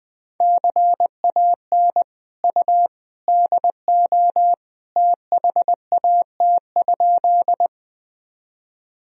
Morse code, heard as a keyed tone, about 21 wpm, 700 Hz. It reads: CAN U DO THAT?
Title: Sentences from Top 100 Words and Abbreviations 20wpm